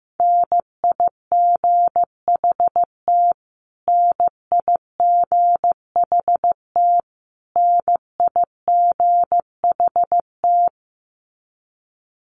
Morse code, heard as a keyed tone, about 15 wpm, 700 Hz. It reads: NIGHT NIGHT NIGHT
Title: Top 500 Words Repeated 15wpm